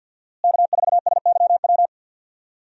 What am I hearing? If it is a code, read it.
Morse code, heard as a keyed tone, about 47 wpm, 700 Hz. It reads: K4SYW
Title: USA Call Signs - 5 Characters - Repeated 3x - 50wpm